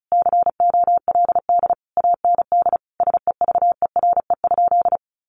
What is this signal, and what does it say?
Morse code, heard as a keyed tone, about 34 wpm, 700 Hz. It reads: COLD AND SEVERE?
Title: Sentences from Top 1400 Words 35wpm